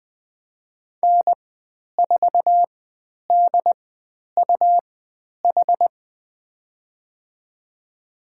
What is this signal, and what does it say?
Morse code, heard as a keyed tone, about 20 wpm, spaced as for 10 wpm, 700 Hz. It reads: N4DUH
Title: 102 Lesson 17g - Letter u - Call Signs 20wpm